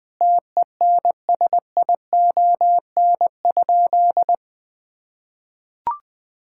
Morse code, heard as a keyed tone, about 20 wpm, 700 Hz. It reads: TENSION?
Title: Sentences from Top 1600 Words 20wpm